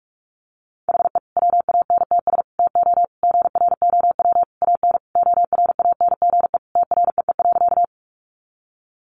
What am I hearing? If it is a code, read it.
Morse code, heard as a keyed tone, about 44 wpm, 700 Hz. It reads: HE WANTS TO GROW AN ORANGE TREE.
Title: Easy Phrases 45wpm